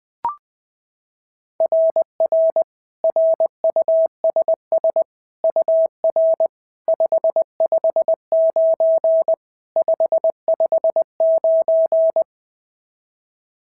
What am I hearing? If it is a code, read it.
Morse code, heard as a keyed tone, about 20 wpm, 690 Hz. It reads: RR RUSS UR 559 559
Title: SOTA QSO Practice 20wpm